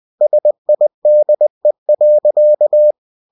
Morse code, heard as a keyed tone, about 20 wpm, 600 Hz. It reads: SIDE.
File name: cw-books / winnie-the-pooh / winnie-20wpm-page-0044